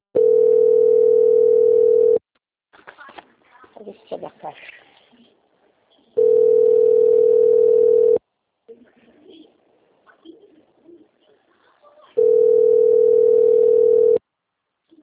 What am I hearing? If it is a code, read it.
Ringback tone (North America), 440 + 480 Hz, with speech.
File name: out- - - - - .20533